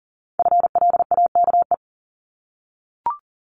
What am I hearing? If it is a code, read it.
Morse code, heard as a keyed tone, about 41 wpm, 700 Hz. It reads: FLAKE
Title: Multi-letter Phonemes 40wpm